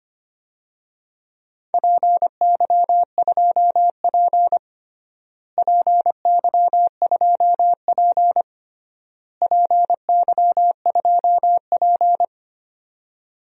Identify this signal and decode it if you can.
Morse code, heard as a keyed tone, about 25 wpm, 700 Hz. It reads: PY2P PY2P PY2P
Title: DXCC-100 Call Signs - 4 Characters - Repeated 3x - 25wpm